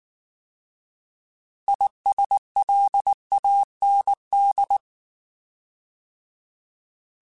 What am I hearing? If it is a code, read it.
Morse code, heard as a keyed tone, about 19 wpm, 800 Hz. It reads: ISLAND